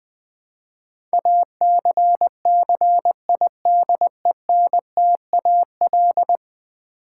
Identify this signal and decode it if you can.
Morse code, heard as a keyed tone, about 20 wpm, 700 Hz. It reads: ACCIDENTAL
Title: Build A Word - 5 Letters At A Time - 12-Letter Words - 20wpm